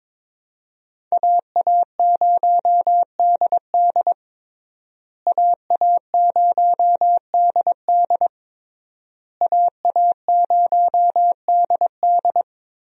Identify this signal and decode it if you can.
Morse code, heard as a keyed tone, about 21 wpm, 700 Hz. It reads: AA0DD AA0DD AA0DD
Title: USA Call Signs - 5 Characters - Repeated 3x - 22wpm